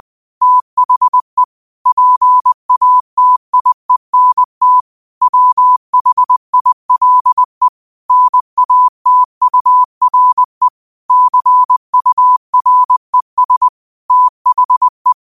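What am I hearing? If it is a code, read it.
Morse code, heard as a keyed tone, about 20 wpm, 1000 Hz. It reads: THE PATIENT WHILE NATURE CURES THE
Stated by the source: Morse code quote of the day at 20 wpm.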